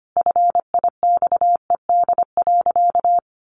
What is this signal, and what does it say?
Morse code, heard as a keyed tone, about 25 wpm, 700 Hz. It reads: FIXED.
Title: Sentences from Top 1100 Words 25wpm